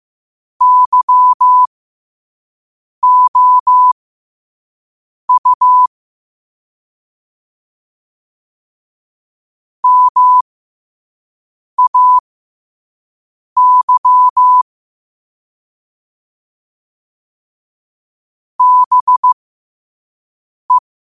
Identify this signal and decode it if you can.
Morse code, heard as a keyed tone, about 15 wpm, spaced as for 5 wpm, 1000 Hz. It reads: YOU MAY BE
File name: qotdfiles / 5wpm